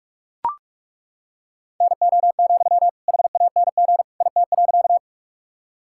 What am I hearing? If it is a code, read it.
Morse code, heard as a keyed tone, about 41 wpm, 700 Hz. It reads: NO, HANG IT.
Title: Sentences from Top 1600 Words 45wpm